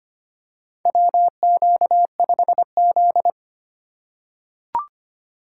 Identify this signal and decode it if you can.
Morse code, heard as a keyed tone, about 25 wpm, 700 Hz. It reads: WQ5Z
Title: USA Call Signs - 3x Character Spacing - 25wpm